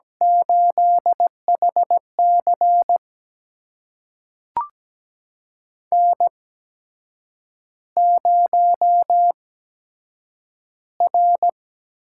Morse code, heard as a keyed tone, about 17 wpm, spaced as for 7 wpm, 700 Hz. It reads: 8HC N0R
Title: USA Call Signs - 4 Characters - 8x Character Spacing - 17wpm